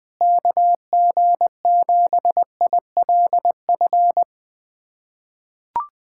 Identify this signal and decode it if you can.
Morse code, heard as a keyed tone, about 20 wpm, 700 Hz. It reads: KG7ILF
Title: USA Callsign DE Callsign 20wpm